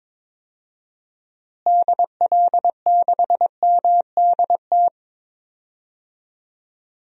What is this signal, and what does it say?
Morse code, heard as a keyed tone, about 22 wpm, 700 Hz. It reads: DL6MDT
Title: DXCC-100 Call Signs - 6 Characters - Repeated 2x - 22wpm